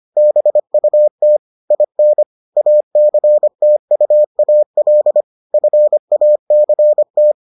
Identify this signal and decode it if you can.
Morse code, heard as a keyed tone, about 25 wpm, 600 Hz. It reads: BUT IN ACTUAL FACT